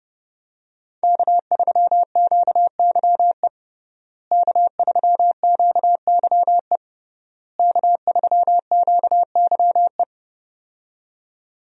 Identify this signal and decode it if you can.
Morse code, heard as a keyed tone, about 30 wpm, 700 Hz. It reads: K3QYE K3QYE K3QYE
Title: USA Call Signs - 5 Characters - Repeated 3x - 30wpm